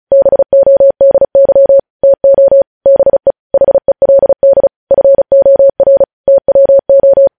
Morse code, heard as a keyed tone, about 36 wpm, 550 Hz. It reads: BODY TO BE HELD FOR TWO